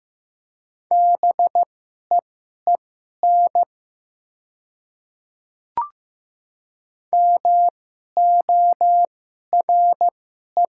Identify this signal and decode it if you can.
Morse code, heard as a keyed tone, about 15 wpm, spaced as for 11 wpm, 700 Hz. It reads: BEEN MORE
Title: 2x Character Spacing - Top 300 Words - 15wpm